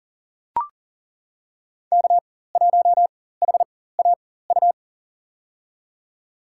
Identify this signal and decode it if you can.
Morse code, heard as a keyed tone, about 40 wpm, spaced as for 19 wpm, 700 Hz. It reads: K1HAU
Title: USA Call Signs - 4x Character Spacing - 40wpm